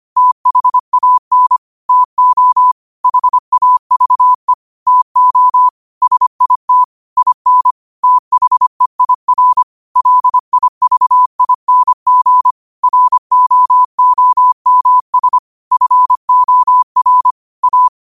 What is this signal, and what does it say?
Morse code, heard as a keyed tone, about 25 wpm, 1000 Hz. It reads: THAN TO HAVE TO SIT IN THEIR LIVING ROOMS FOR A